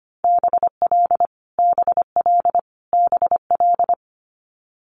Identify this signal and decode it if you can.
Morse code, heard as a keyed tone, about 25 wpm, 700 Hz. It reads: BL BL BL
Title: Multi-letter Phonemes Repeated 3x 25wpm